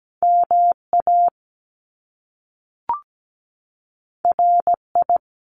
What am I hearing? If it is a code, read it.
Morse code, heard as a keyed tone, about 17 wpm, 700 Hz. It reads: MA RI
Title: US States 17wpm